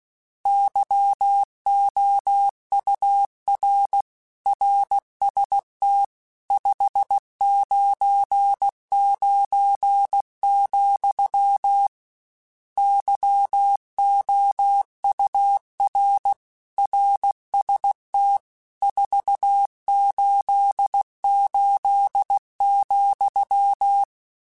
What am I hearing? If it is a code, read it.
Morse code, heard as a keyed tone, about 16 wpm, 800 Hz. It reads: YOUR RST 599, YOUR RST 488,
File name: RSTs / RST-mp3s / Your RSTs @15